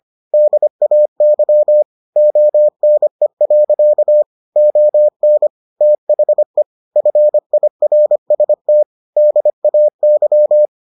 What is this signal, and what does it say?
Morse code, heard as a keyed tone, about 25 wpm, 600 Hz. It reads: DAY ONE. ON THE FIRST DAY